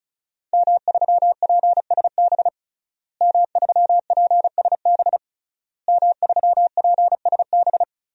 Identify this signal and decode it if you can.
Morse code, heard as a keyed tone, about 36 wpm, 700 Hz. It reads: M3PSB M3PSB M3PSB
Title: DXCC-100 Call Signs - 5 Characters - Repeated 3x - 35wpm